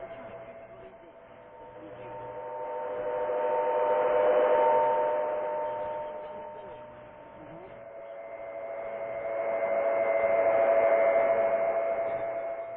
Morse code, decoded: none

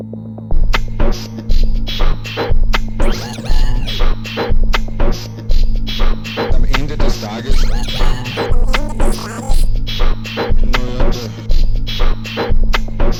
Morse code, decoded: none

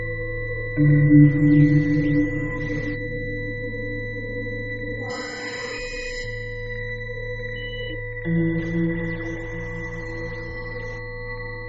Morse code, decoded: none

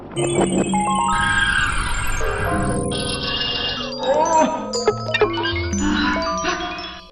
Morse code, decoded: none

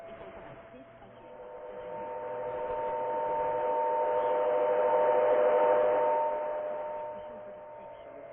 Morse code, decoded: none

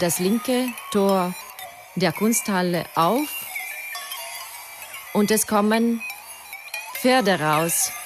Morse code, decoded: none

